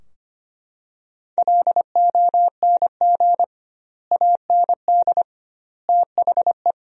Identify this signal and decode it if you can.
Morse code, heard as a keyed tone, about 25 wpm, 700 Hz. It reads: LONG AND THE
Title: 2x Word-Spacing - Sentences from Top 400 Words - 25wpm